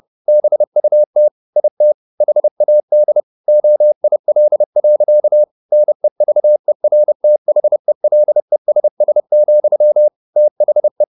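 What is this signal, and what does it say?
Morse code, heard as a keyed tone, about 30 wpm, 600 Hz. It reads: BUT IT HAD OIL. NEVERTHELESS, THE
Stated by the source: Morse code at 30 wpm, the novel Never by Ken Follett in English, from CW med Gnister.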